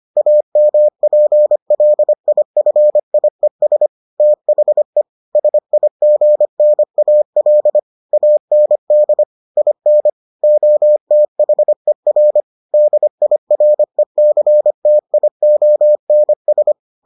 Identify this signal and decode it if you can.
Morse code, heard as a keyed tone, about 25 wpm, 600 Hz. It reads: AMPLIFIES THE SIGNAL AND IN OTHER DIRECTIONS